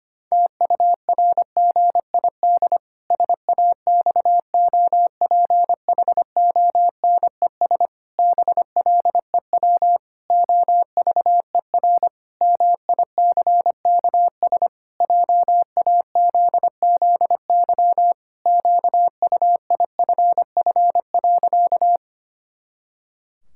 Morse code, heard as a keyed tone, about 25 wpm, 700 Hz. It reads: TURGID SAXOPHONES BLEW OVER MICKS JAZZY QUIFF.
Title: Pangram Challenge v2 - 25wpm